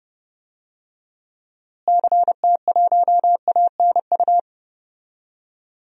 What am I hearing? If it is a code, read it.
Morse code, heard as a keyed tone, about 30 wpm, 700 Hz. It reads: CT1ANU
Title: DXCC-100 Call Signs - 6 Characters - Repeated 3x - 30wpm